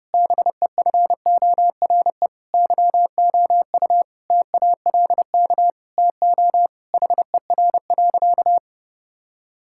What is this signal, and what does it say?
Morse code, heard as a keyed tone, about 30 wpm, 700 Hz. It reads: BEFORE YOU TALK TO HER.
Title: Sentences from Top 1000 Words - Review Part 6 - 30wpm